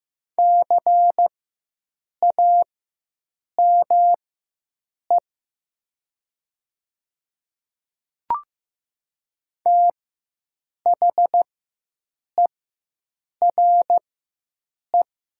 Morse code, heard as a keyed tone, about 15 wpm, spaced as for 7 wpm, 700 Hz. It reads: CAME THERE
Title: 4x Character Spacing - Top 300 Words - 15wpm